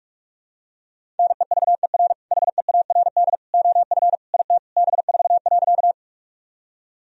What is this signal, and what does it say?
Morse code, heard as a keyed tone, about 42 wpm, 700 Hz. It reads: NEVER HEARD OF IT B4.
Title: Sentences from Top 400 Words and Abbreviations 45wpm